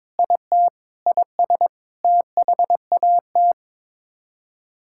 Morse code, heard as a keyed tone, about 21 wpm, 700 Hz. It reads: IT IS THAT
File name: practice / 22wpm / Sets of 8 Words - 22wpm